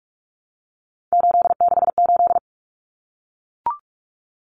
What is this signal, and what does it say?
Morse code, heard as a keyed tone, about 44 wpm, 700 Hz. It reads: 868
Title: Numbers - 3-Digit 45wpm